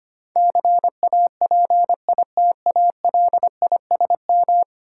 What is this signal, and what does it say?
Morse code, heard as a keyed tone, about 25 wpm, 700 Hz. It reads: CAPITALISM
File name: Ten Letter Words - 25wpm